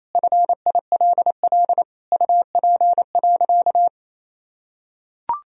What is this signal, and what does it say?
Morse code, heard as a keyed tone, about 28 wpm, 700 Hz. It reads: FILL UP.
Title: Sentences from Top 1000 Words - Review Part 3 - 28wpm